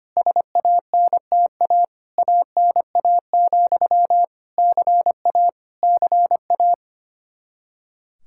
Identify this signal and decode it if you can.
Morse code, heard as a keyed tone, about 25 wpm, 700 Hz. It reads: SANTA ANA, CA CA
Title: Practice QSO - Exchange 1 - Beginning - 25wpm